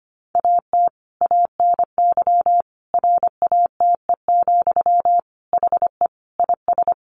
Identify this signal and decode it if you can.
Morse code, heard as a keyed tone, about 25 wpm, 700 Hz. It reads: AT ANY RATE, HE IS
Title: Sentences from Top 900 Words 25wpm